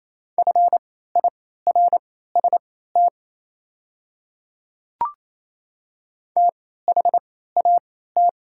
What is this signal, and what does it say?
Morse code, heard as a keyed tone, about 27 wpm, spaced as for 16 wpm, 700 Hz. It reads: FIRST THAT
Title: 3x Character Spacing - Top 200 Words - 28wpm